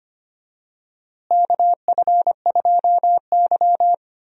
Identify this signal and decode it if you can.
Morse code, heard as a keyed tone, about 25 wpm, 700 Hz. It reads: KF2Y
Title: USA Call Signs - 4 Characters - Repeated 2x - 25wpm